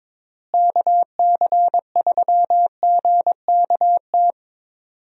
Morse code, heard as a keyed tone, about 22 wpm, 700 Hz. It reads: KC3GKT
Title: USA Call Signs - 6 Characters - Repeated 3x - 22wpm